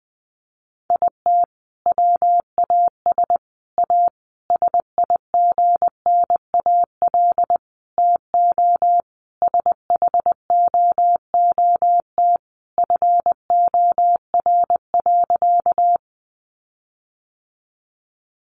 Morse code, heard as a keyed tone, about 20 wpm, 700 Hz. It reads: IT WAS A SIGNAL TO SHOOT FOR.
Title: Sentences from Top 1400 Words 20wpm